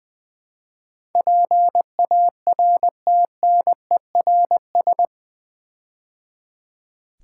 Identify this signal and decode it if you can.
Morse code, heard as a keyed tone, about 20 wpm, 700 Hz. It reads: PARTNERS
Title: Build A Word - 4 Letters At A Time - 11-Letter Words - 20wpm